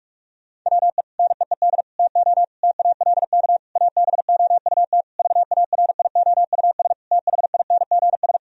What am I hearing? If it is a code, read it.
Morse code, heard as a keyed tone, about 46 wpm, 700 Hz. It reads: WE NEED TO TALK ABOUT VARIOUS THINGS
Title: Sentences from Top 1000 Words - Review Part 4 - 45wpm